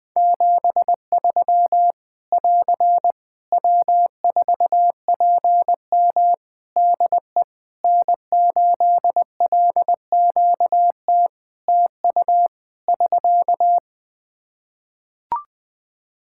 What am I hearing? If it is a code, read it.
Morse code, heard as a keyed tone, about 21 wpm, 700 Hz. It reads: 73 <AR> W4PM DE N8LQT TU <SK>
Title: Practice QSO - Exchange 4 - Ending - 20wpm